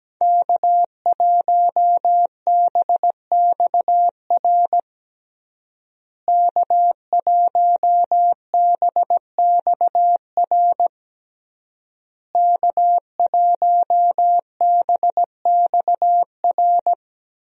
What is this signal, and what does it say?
Morse code, heard as a keyed tone, about 17 wpm, 700 Hz. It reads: K1BXR K1BXR K1BXR
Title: USA Call Signs - 5 Characters - Repeated 3x - 17wpm